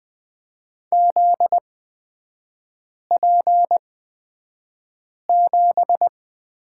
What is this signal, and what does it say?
Morse code, heard as a keyed tone, about 20 wpm, spaced as for 5 wpm, 700 Hz. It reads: ZP7
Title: 243 Lesson 39e - Letter z - Three Characters 20wpm